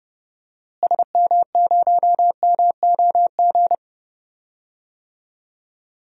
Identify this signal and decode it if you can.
Morse code, heard as a keyed tone, about 30 wpm, 700 Hz. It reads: SM0MOG
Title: DXCC-100 Call Signs - 6 Characters - Repeated 2x - 30wpm